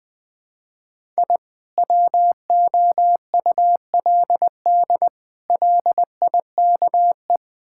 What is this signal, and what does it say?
Morse code, heard as a keyed tone, about 20 wpm, 700 Hz. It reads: I WOULD LIKE
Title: Build A Phrase - 5-Word Phrases - 20wpm